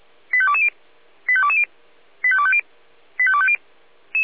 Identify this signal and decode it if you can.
Five-tone selective call (ZVEI), heard as quick runs of five tones, tones of about 70 ms. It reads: 85220 85220 85250 85250